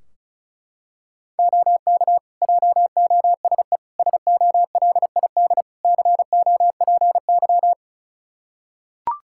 Morse code, heard as a keyed tone, about 36 wpm, 700 Hz. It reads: OK JOSE SOLID COPY
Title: K1USN SST QSO Practice 35wpm